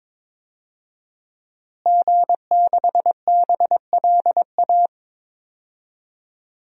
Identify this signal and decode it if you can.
Morse code, heard as a keyed tone, about 22 wpm, 700 Hz. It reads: G6BLA
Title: DXCC-100 Call Signs - 5 Characters - Repeated 2x - 22wpm